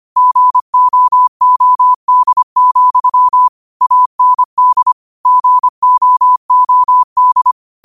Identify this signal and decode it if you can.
Morse code, heard as a keyed tone, about 25 wpm, 1000 Hz. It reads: GOOD, AND GOOD
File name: qotdfiles / 25wpm